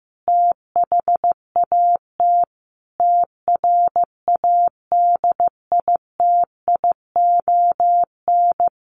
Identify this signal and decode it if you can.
Morse code, heard as a keyed tone, about 15 wpm, 700 Hz. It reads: THAT TRADITION
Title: Sentences from Top 1500 Words 15wpm